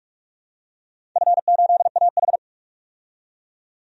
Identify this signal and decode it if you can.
Morse code, heard as a keyed tone, about 42 wpm, 700 Hz. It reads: F8AH
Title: DXCC-100 Call Signs - 4 Characters - 45wpm